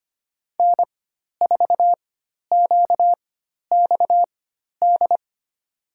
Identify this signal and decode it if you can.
Morse code, heard as a keyed tone, about 25 wpm, spaced as for 12 wpm, 700 Hz. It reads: N4QXD
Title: USA Call Signs - 5 Characters - 4x Character Spacing - 25wpm